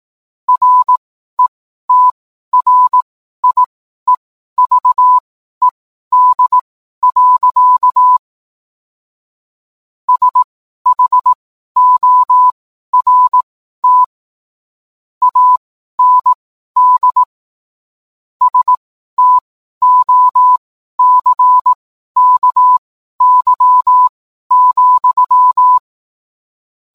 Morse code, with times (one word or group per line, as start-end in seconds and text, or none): 0.5-8.2 RETRIEVED.
10.1-14.0 SHORT
15.2-17.2 AND
18.4-25.8 STOCKY,